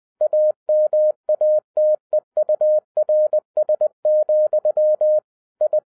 Morse code, heard as a keyed tone, about 20 wpm, 610 Hz. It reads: AMATEURS, I